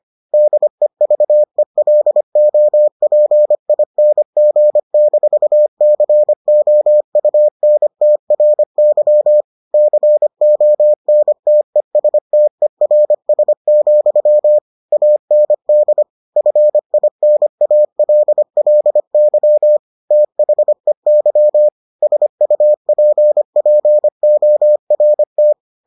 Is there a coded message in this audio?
Yes: Morse code from 0.3 to 25.5 s, heard as a keyed tone, about 25 wpm, 600 Hz. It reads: DEVELOPING-COUNTRY CONTESTERS, AND FINALLY THEY SUPPORT